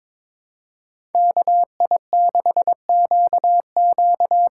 Morse code, heard as a keyed tone, about 22 wpm, 700 Hz. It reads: KI6QQ
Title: USA Call Signs - 8x Character Spacing - 22wpm